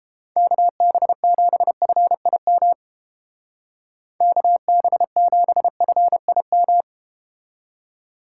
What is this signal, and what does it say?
Morse code, heard as a keyed tone, about 31 wpm, 700 Hz. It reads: KB7FIM KB7FIM